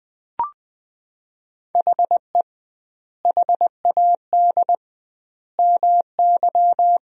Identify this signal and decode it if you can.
Morse code, heard as a keyed tone, about 20 wpm, 700 Hz. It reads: HE HAD MY
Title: 2x Word-Spacing - Sentences from Top 400 Words - 20wpm